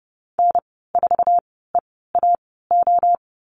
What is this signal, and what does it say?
Morse code, heard as a keyed tone, about 30 wpm, spaced as for 17 wpm, 700 Hz. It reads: N4EAO
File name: Lesson 15g - Number 5 - Call Signs 30wpm